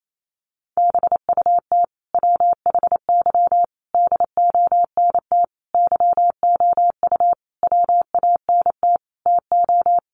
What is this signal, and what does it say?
Morse code, heard as a keyed tone, about 28 wpm, 700 Hz. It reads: BUT WHY DONT YOU WANT TO
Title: Sentences from Top 300 Words 28wpm